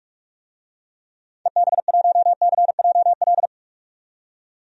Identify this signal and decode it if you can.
Morse code, heard as a keyed tone, about 41 wpm, 700 Hz. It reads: EB1CJL
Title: DXCC-100 Call Signs - 6 Characters - 45wpm